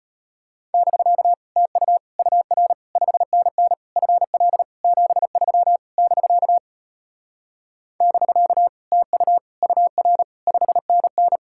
Morse code, heard as a keyed tone, about 33 wpm, 700 Hz. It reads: <BK> TU UR 5NN FL 73 <BK> <BK> TU UR 5NN